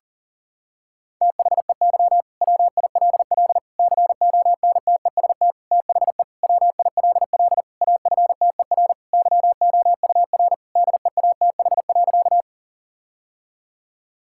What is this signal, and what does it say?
Morse code, heard as a keyed tone, about 40 wpm, 700 Hz. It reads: THEY WILL CONTEST THE WILL AFTER YOUR DEATH.
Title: Sentences from Top 1600 Words 40wpm